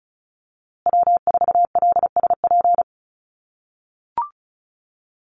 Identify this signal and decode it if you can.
Morse code, heard as a keyed tone, about 34 wpm, 700 Hz. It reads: W4LSP
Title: USA Call Signs Repeated 2x 35wpm